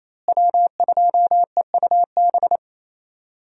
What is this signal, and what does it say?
Morse code, heard as a keyed tone, about 28 wpm, 700 Hz. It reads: W2EUB
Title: USA Call Signs - 5 Characters - Repeated 2x - 28wpm